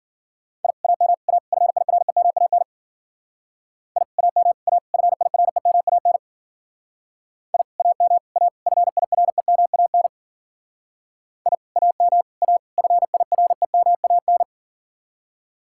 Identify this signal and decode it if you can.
Morse code, heard as a keyed tone, about 49 wpm, 700 Hz. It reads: I AM A FIREMAN I AM A FIREMAN I AM A FIREMAN I AM A FIREMAN
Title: Speed-Racing Common QSO Phrases - 40wpm